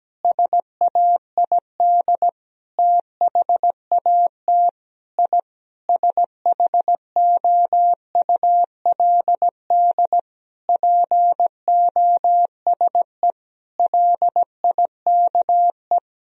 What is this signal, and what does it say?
Morse code, heard as a keyed tone, about 17 wpm, 700 Hz. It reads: SAID THAT I SHOULD POSE LIKE
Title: Sentences from Top 1000 Words - Review Part 2 - 17wpm